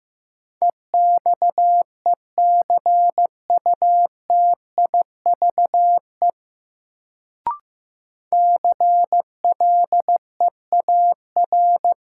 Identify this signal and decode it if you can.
Morse code, heard as a keyed tone, about 15 wpm, 700 Hz. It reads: EXECUTIVE CLEAR
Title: Sets of 5 Words Encore - New Word Drill - 15wpm